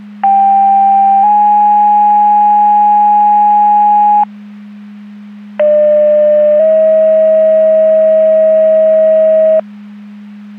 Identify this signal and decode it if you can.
Two-tone page, heard as a one-second tone, then a three-second tone, about 790 then 830 Hz, about 600 then 640 Hz. It is heard from 0.2 to 9.6 s.
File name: Portage County 154.3850 Fire EMS Paging